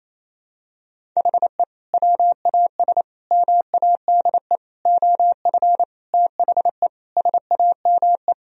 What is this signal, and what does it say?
Morse code, heard as a keyed tone, about 28 wpm, 700 Hz. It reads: HE WAS MADE OF THE SAME
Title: Sentences from Top 1000 Words - Review Part 4 - 28wpm